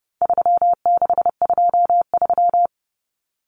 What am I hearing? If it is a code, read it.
Morse code, heard as a keyed tone, about 28 wpm, 700 Hz. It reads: 3623
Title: Numbers - 4-Digit 30wpm